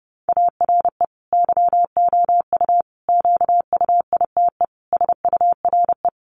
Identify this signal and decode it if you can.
Morse code, heard as a keyed tone, about 30 wpm, 700 Hz. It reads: ARE YOU QUITE SURE
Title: Sentences from Top 700 Words 30wpm